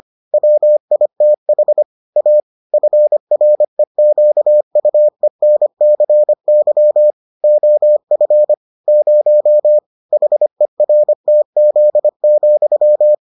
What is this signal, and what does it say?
Morse code, heard as a keyed tone, about 25 wpm, 600 Hz. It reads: WITH A FREQUENCY OF 0 HERTZ,